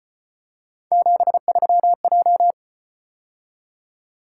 Morse code, heard as a keyed tone, about 32 wpm, 700 Hz. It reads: 73J